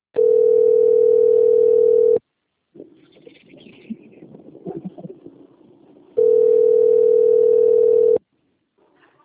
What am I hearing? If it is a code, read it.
Ringback tone (North America), 440 + 480 Hz.